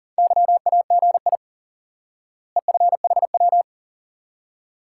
Morse code, heard as a keyed tone, about 40 wpm, 700 Hz. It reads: YAGI EFHW
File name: IQE Course 3 - Lesson 89 - YAGI - Two Elements - 40wpm